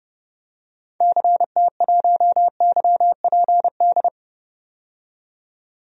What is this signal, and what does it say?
Morse code, heard as a keyed tone, about 30 wpm, 700 Hz. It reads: CT1YPD
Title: DXCC-100 Call Signs - 6 Characters - 30wpm